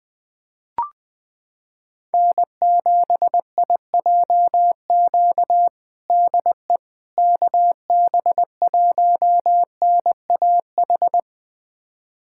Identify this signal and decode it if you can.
Morse code, heard as a keyed tone, about 20 wpm, 700 Hz. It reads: N7IJQ DE KB1NAH
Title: USA Callsign DE Callsign 20wpm